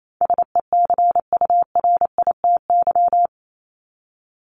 Morse code, heard as a keyed tone, about 28 wpm, 700 Hz. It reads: SECURITY
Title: Top 501-750 Words 28wpm